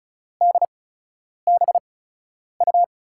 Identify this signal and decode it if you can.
Morse code, heard as a keyed tone, about 36 wpm, spaced as for 10 wpm, 700 Hz. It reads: DBU
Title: USA Call Signs - 6 Characters - 8x Character Spacing - 35wpm